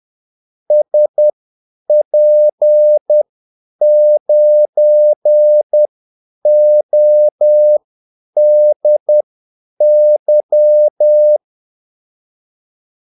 Morse code, heard as a keyed tone, about 10 wpm, spaced as for 8 wpm, 600 Hz. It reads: SP9ODY
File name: Callsigns-10-8wpm